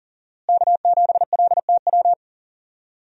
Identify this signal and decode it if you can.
Morse code, heard as a keyed tone, about 40 wpm, 700 Hz. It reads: K7LTW
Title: USA Call Signs - 5 Characters - Repeated 2x - 40wpm